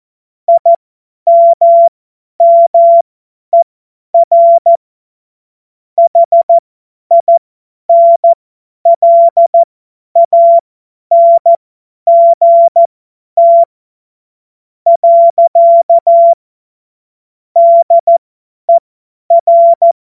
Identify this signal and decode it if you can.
Morse code, heard as a keyed tone, about 14 wpm, spaced as for 10 wpm, 680 Hz. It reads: IMMER HINLANGT. DER
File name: Morsekurs / Witze / Witz 23 Januar 2024 50 F70